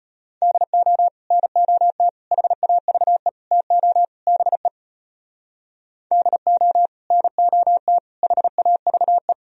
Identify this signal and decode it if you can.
Morse code, heard as a keyed tone, about 33 wpm, 700 Hz. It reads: DO NOT HAVE TO BE DO NOT HAVE